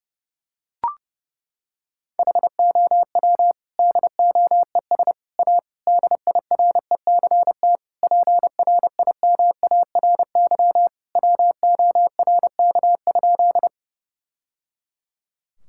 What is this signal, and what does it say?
Morse code, heard as a keyed tone, about 30 wpm, 700 Hz. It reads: HOW DOES A DIRECT PRIMARY WORK?
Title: Sentences from Top 1000 Words - Review Part 5 - 30wpm